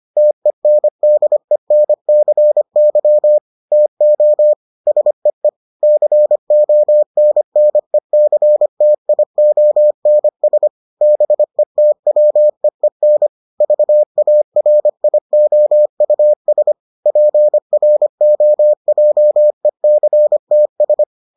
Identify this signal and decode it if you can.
Morse code, heard as a keyed tone, about 25 wpm, 600 Hz. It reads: TENDENCY TO SEE CONNECTIONS BETWEEN VARIOUS PROJECTS